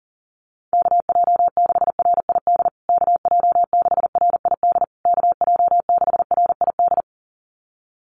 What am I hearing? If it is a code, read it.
Morse code, heard as a keyed tone, about 40 wpm, 700 Hz. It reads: KJ6RID KJ6RID KJ6RID
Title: USA Call Signs Repeated 3x 40wpm